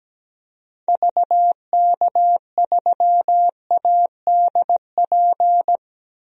Morse code, heard as a keyed tone, about 17 wpm, 700 Hz. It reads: VK3ADP
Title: DXCC-100 Call Signs - 6 Characters - 17wpm